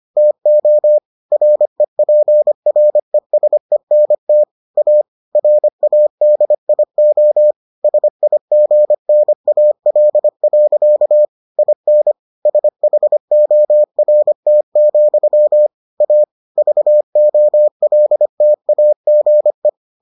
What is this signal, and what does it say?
Morse code, heard as a keyed tone, about 25 wpm, 600 Hz. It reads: TO REPRESENT A RADIO SIGNAL. IN SHORT, A VOLTAGE